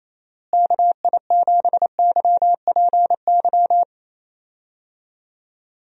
Morse code, heard as a keyed tone, about 28 wpm, 700 Hz. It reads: KI7YPY
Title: USA Call Signs - 6 Characters - Repeated 2x - 28wpm